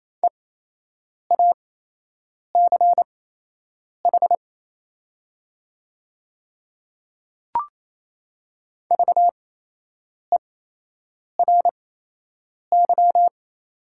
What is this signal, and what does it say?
Morse code, heard as a keyed tone, about 28 wpm, spaced as for 8 wpm, 700 Hz. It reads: EACH VERY
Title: 8x Character Spacing - Top 200 Words - 28wpm